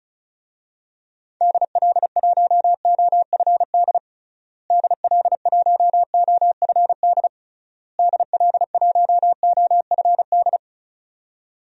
Morse code, heard as a keyed tone, about 34 wpm, 700 Hz. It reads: DL1OFD DL1OFD DL1OFD
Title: DXCC-100 Call Signs - 6 Characters - Repeated 3x - 35wpm